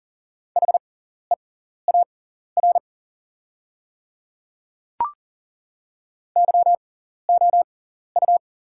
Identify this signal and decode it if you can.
Morse code, heard as a keyed tone, about 39 wpm, spaced as for 14 wpm, 690 Hz. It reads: HEAR YOU
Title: 6x Character Spacing - Top 500 Words - 40wpm